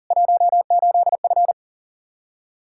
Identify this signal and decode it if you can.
Morse code, heard as a keyed tone, about 40 wpm, 700 Hz. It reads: 18F
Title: Practice QSO - Exchange 2 - Info - 40wpm